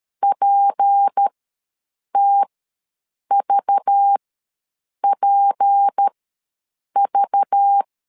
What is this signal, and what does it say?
Morse code, heard as a keyed tone, about 13 wpm, spaced as for 7 wpm, 800 Hz. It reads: PTVPV
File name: LEZIONI / 6 PKX0 / 6 P ws 12wpm 3117